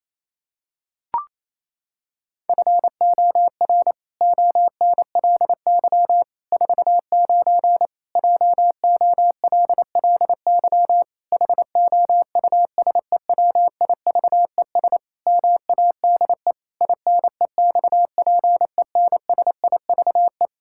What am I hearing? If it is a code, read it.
Morse code, heard as a keyed tone, about 28 wpm, 700 Hz. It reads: FOR ONLY 49 JOLLY HOUSEWIVES MADE INEXPENSIVE
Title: Pangram Challenge v2 - 28wpm